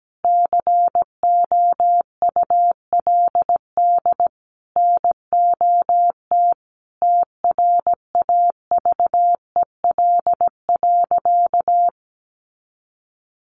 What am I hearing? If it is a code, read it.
Morse code, heard as a keyed tone, about 17 wpm, 700 Hz. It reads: COULD NOT TRAVEL.
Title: Sentences from Top 1300 Words 17wpm